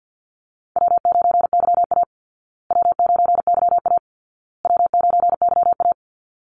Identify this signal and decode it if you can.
Morse code, heard as a keyed tone, about 49 wpm, 700 Hz. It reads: W9YA W9YA W9YA
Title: Call Signs Repeated 2x - 50wpm